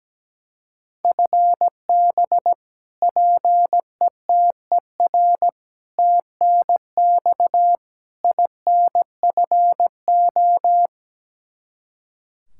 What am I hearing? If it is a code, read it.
Morse code, heard as a keyed tone, about 17 wpm, 700 Hz. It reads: FB PETER TNX INFO
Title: Practice QSO - Exchange 3 - More Info - 17wpm